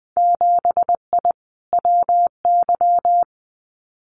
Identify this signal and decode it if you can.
Morse code, heard as a keyed tone, about 20 wpm, 700 Hz. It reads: 7I WY
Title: Winter Field Day Exchanges Encore 20wpm